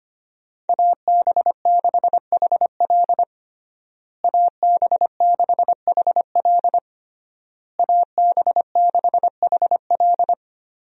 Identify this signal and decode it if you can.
Morse code, heard as a keyed tone, about 25 wpm, 700 Hz. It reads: AB6HL AB6HL AB6HL
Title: USA Call Signs - 5 Characters - Repeated 3x - 25wpm